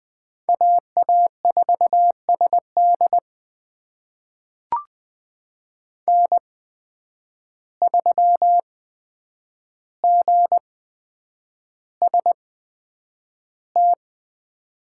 Morse code, heard as a keyed tone, about 21 wpm, spaced as for 9 wpm, 700 Hz. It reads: AA4SD N3GST
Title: USA Call Signs - 5 Characters - 8x Character Spacing - 20wpm